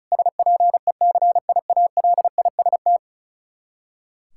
Morse code, heard as a keyed tone, about 34 wpm, 700 Hz. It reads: SPECIALIST
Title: K1USN SST QSO Practice 35wpm